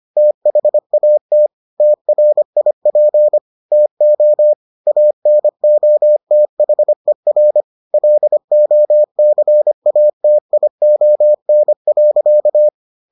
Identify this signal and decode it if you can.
Morse code, heard as a keyed tone, about 25 wpm, 600 Hz. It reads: THAT TRIP TO ANOTHER LOCATION.